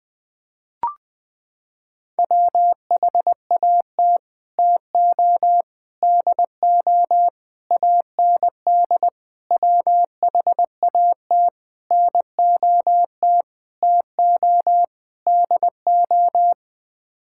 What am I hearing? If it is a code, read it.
Morse code, heard as a keyed tone, about 20 wpm, 700 Hz. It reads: WHAT TO DO AND WHAT NOT TO DO
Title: Sets of 8 Words - 20wpm